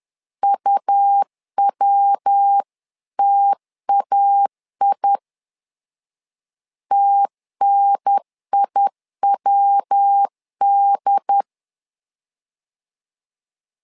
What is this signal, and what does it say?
Morse code, heard as a keyed tone, about 10 wpm, 800 Hz. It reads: UWTAI TNIWD
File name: LEZIONI / 3 MWRO / 3 MW 10wpm 1959